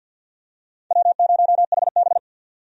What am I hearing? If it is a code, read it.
Morse code, heard as a keyed tone, about 50 wpm, 700 Hz. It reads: W0HB
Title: USA Call Signs - 4 Characters - 50wpm